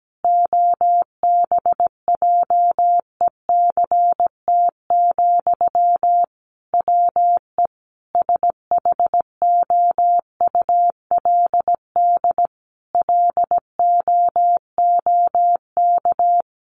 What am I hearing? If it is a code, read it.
Morse code, heard as a keyed tone, about 17 wpm, 700 Hz. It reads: OBJECT, WE SHOULD LOOK
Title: Sentences from Top 800 Words 17wpm